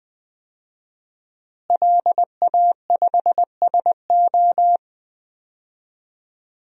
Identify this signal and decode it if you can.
Morse code, heard as a keyed tone, about 20 wpm, 700 Hz. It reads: LA5SO
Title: DXCC-100 Call Signs - 5 Characters - Repeated 2x - 20wpm